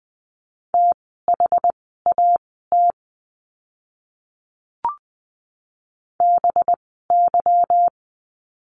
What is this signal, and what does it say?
Morse code, heard as a keyed tone, about 20 wpm, spaced as for 14 wpm, 700 Hz. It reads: THAT BY
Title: 206 Lesson 33f - Letter j - Words 20wpm